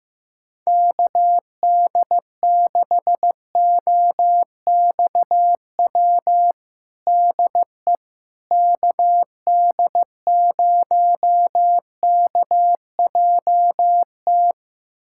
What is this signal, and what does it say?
Morse code, heard as a keyed tone, about 15 wpm, 700 Hz. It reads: KD6OXW DE KD0KJT
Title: USA Callsign DE Callsign - 6 Characters - 15wpm